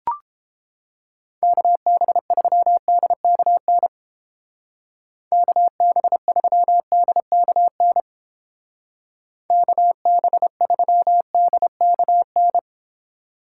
Morse code, heard as a keyed tone, about 29 wpm, 700 Hz. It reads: KB3DKN KB3DKN KB3DKN